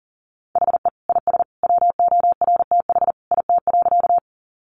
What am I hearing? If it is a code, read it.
Morse code, heard as a keyed tone, about 40 wpm, 700 Hz. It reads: HE IS WORTH IT.
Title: Sentences from Top 800 Words 40wpm